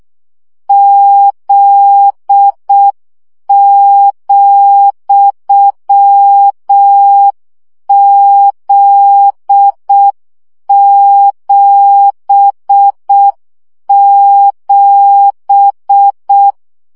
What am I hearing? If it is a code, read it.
Morse code, heard as a keyed tone, about 6 wpm, 800 Hz. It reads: Z,Z77
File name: CW gyakorlas z 7 vesszo 5 baud